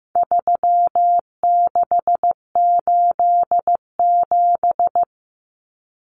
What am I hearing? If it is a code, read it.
Morse code, heard as a keyed tone, about 15 wpm, 700 Hz. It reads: 3687
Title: Numbers - 4-Digit 15wpm